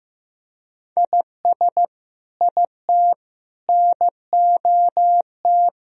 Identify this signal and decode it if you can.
Morse code, heard as a keyed tone, about 15 wpm, 700 Hz. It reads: IS IT NOT
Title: Sentences from Top 1000 Words - Review Part 2 - 15wpm